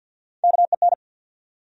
Morse code, heard as a keyed tone, about 47 wpm, 700 Hz. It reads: KEN
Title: USA Common Names - Max 3 Letters - 50wpm